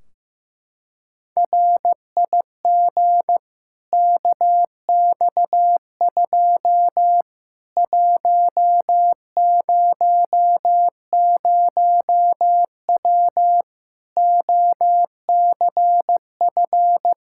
Morse code, heard as a keyed tone, about 15 wpm, 700 Hz. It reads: RIG KX2 100W OCF
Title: Practice QSO - Exchange 2 - Info - 15wpm